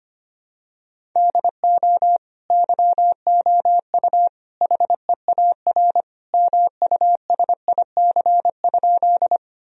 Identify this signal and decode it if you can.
Morse code, heard as a keyed tone, about 25 wpm, 700 Hz. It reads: DO YOU HEAR MUSIC?
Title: Sentences from Top 1000 Words - Review Part 1 - 25wpm